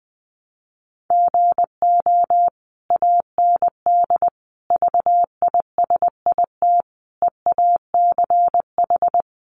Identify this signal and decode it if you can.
Morse code, heard as a keyed tone, about 20 wpm, 700 Hz. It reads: GO AND VISIT EACH
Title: Sentences from Top 1100 Words 20wpm